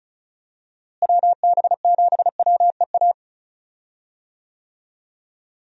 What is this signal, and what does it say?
Morse code, heard as a keyed tone, about 36 wpm, 700 Hz. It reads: WB7WEA
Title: USA Call Signs - 6 Characters - Repeated 2x - 35wpm